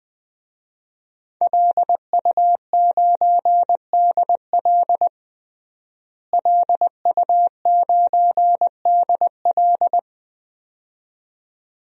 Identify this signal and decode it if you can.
Morse code, heard as a keyed tone, about 20 wpm, 700 Hz. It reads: LU9DL LU9DL
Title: DXCC-100 Call Signs - 5 Characters - Repeated 2x - 20wpm